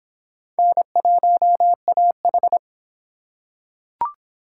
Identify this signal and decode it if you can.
Morse code, heard as a keyed tone, about 26 wpm, 700 Hz. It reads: N1AH